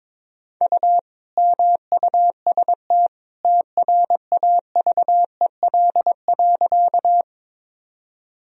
Morse code, heard as a keyed tone, about 22 wpm, 700 Hz. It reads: U MUST TRAVEL.
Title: Sentences from Top 400 Words and Abbreviations 22wpm